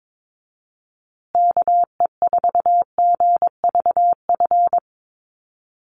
Morse code, heard as a keyed tone, about 22 wpm, 700 Hz. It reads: KE4GVF
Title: USA Call Signs Repeated 3x 22wpm